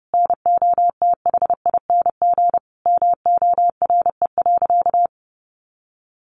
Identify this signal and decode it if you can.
Morse code, heard as a keyed tone, about 30 wpm, 700 Hz. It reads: NOTHING MORE.
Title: Sentences from Top 1400 Words 30wpm